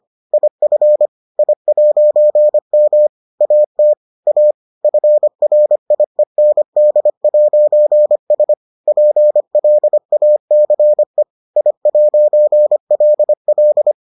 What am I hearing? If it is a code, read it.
Morse code, heard as a keyed tone, about 25 wpm, 600 Hz. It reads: IF I'M AT A FRIEND'S PLACE I'LL